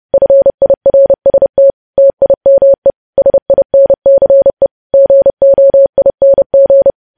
Morse code, heard as a keyed tone, about 30 wpm, 550 Hz. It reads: FIRST TIME SINCE GOING
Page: News Headlines in Morse Code at 30 WPM